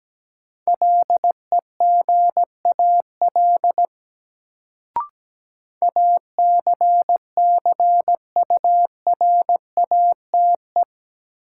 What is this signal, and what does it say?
Morse code, heard as a keyed tone, about 17 wpm, 700 Hz. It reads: LEGAL ACCURATE
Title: Sentences from Top 1200 Words - New Word Drill - 17wpm